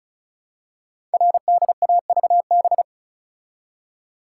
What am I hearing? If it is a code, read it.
Morse code, heard as a keyed tone, about 36 wpm, 700 Hz. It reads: RDAVB
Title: Code Groups - 5 Letters - 35wpm